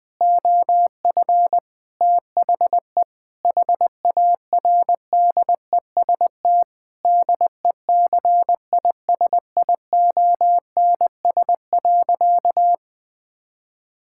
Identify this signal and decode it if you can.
Morse code, heard as a keyed tone, about 21 wpm, 700 Hz. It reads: OF THE HARDEST DECISIONS.
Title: Sentences with 5-Word Phrases - 20wpm